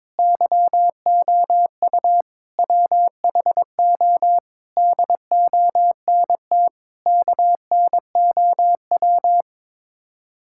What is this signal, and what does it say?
Morse code, heard as a keyed tone, about 22 wpm, 700 Hz. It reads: YOU WHO DONT KNOW
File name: practice / 22wpm / Sets of 7 Words - 22wpm